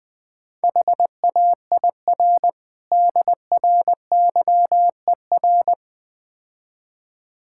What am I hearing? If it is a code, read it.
Morse code, heard as a keyed tone, about 20 wpm, 690 Hz. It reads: HAIR DRYER
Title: Difficult Words - Speed-Racing with Hints - 20wpm